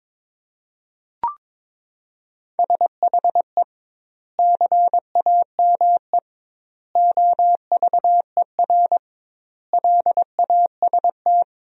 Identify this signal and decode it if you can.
Morse code, heard as a keyed tone, about 22 wpm, 700 Hz. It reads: SHE CAME OVER LAST
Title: 2x Word-Spacing - Sentences from Top 300 Words - 22wpm